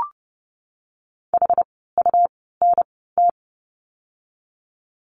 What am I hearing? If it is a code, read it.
Morse code, heard as a keyed tone, about 30 wpm, spaced as for 17 wpm, 700 Hz. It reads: HUNT
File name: Lesson 17f - Letter u - Words 30wpm